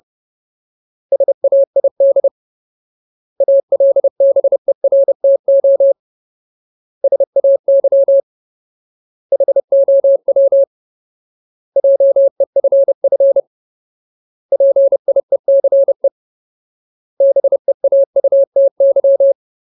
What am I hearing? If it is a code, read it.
Morse code, heard as a keyed tone, about 30 wpm, 550 Hz. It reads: SAID ALBERTO SAY HOW JEFF PIECE BEAUTY
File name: Ultimate Word List / 30WPM Ultimate